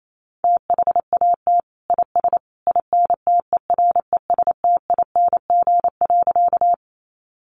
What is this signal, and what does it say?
Morse code, heard as a keyed tone, about 28 wpm, 700 Hz. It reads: THAT IS INTERESTING.
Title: Sentences from Top 1000 Words 28wpm